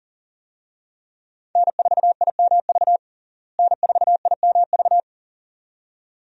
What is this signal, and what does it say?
Morse code, heard as a keyed tone, about 40 wpm, 700 Hz. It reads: N4IMV N4IMV
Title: USA Call Signs - 5 Characters - Repeated 2x - 40wpm